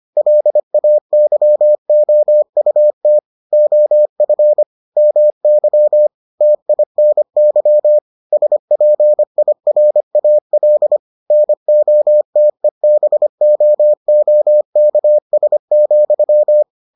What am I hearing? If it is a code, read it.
Morse code, heard as a keyed tone, about 25 wpm, 600 Hz. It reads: LAYOUT OF MY TINY SPIRAL NOTEBOOKS,